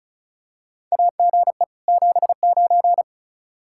Morse code, heard as a keyed tone, about 36 wpm, 700 Hz. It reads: AGE 79
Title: Practice QSO - Exchange 3 - More Info - 35wpm